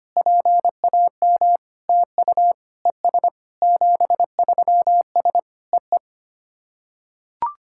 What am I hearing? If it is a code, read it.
Morse code, heard as a keyed tone, about 25 wpm, 700 Hz. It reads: PAM TU ES 73S EE
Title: SOTA QSO Practice 25wpm